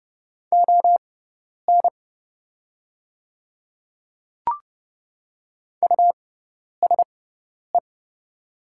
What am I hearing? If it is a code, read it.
Morse code, heard as a keyed tone, about 30 wpm, spaced as for 10 wpm, 700 Hz. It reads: ON USE
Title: 6x Character Spacing - Top 200 Words - 30wpm